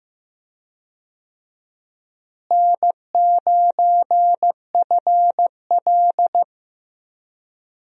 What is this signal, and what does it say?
Morse code, heard as a keyed tone, about 15 wpm, 700 Hz. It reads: N9FL
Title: USA Call Signs - 4 Characters - Repeated 3x - 15wpm